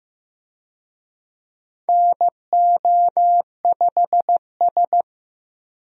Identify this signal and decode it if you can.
Morse code, heard as a keyed tone, about 15 wpm, 700 Hz. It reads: NO5S
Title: USA Call Signs - 4 Characters - Repeated 2x - 15wpm